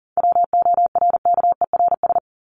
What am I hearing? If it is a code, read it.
Morse code, heard as a keyed tone, about 40 wpm, 700 Hz. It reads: WORKERS
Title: ES:S Suffix - Top 100 Words with Base Words 40wpm